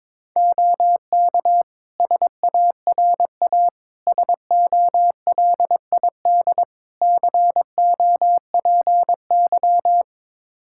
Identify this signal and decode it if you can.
Morse code, heard as a keyed tone, about 22 wpm, 700 Hz. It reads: OK SARA SOLID COPY
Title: Practice QSO - Exchange 3 - More Info - 22wpm